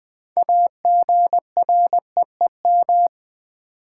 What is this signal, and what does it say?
Morse code, heard as a keyed tone, about 20 wpm, 700 Hz. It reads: AGREEM
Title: Build A Word - 3 Letters At A Time - 9-Letter Words - 20wpm